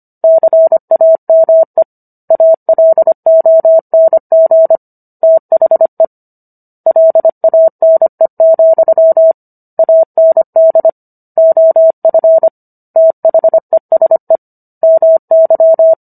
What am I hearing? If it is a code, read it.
Morse code, heard as a keyed tone, about 25 wpm, 650 Hz. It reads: CAME ALONG THE LANE, AND OF THESE MY